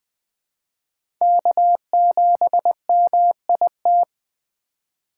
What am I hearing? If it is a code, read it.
Morse code, heard as a keyed tone, about 20 wpm, 700 Hz. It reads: K7MIT
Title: USA Call Signs - 5 Characters - 3x Character Spacing - 20wpm